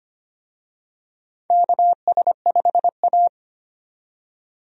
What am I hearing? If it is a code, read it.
Morse code, heard as a keyed tone, about 25 wpm, 700 Hz. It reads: KS5A
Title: USA Call Signs - 4 Characters - 25wpm